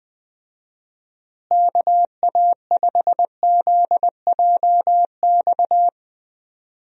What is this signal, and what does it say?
Morse code, heard as a keyed tone, about 20 wpm, 700 Hz. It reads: KA5ZJX
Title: USA Call Signs - 6 Characters - Repeated 3x - 20wpm